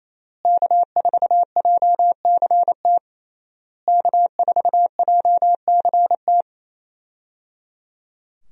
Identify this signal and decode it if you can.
Morse code, heard as a keyed tone, about 28 wpm, 700 Hz. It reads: K4JCT K4JCT
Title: USA Call Signs - 5 Characters - Repeated 3x - 28wpm